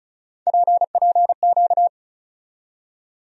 Morse code, heard as a keyed tone, about 34 wpm, 700 Hz. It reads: PPQ
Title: USA Call Sign Postfixes - 3 Letters 35wpm